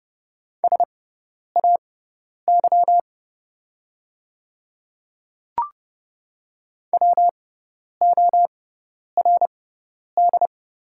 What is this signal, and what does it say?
Morse code, heard as a keyed tone, about 30 wpm, spaced as for 10 wpm, 700 Hz. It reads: SAY WORD
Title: 6x Character Spacing - Top 200 Words - 30wpm